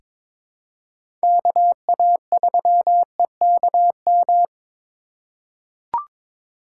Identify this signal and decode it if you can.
Morse code, heard as a keyed tone, about 22 wpm, 700 Hz. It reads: KA3EKM
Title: USA Call Signs - 6 Characters - 4x Character Spacing - 22wpm